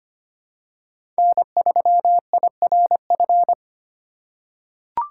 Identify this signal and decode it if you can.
Morse code, heard as a keyed tone, about 25 wpm, 700 Hz. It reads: N3IRF
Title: USA Call Signs - 5 Characters - 6x Character Spacing - 25wpm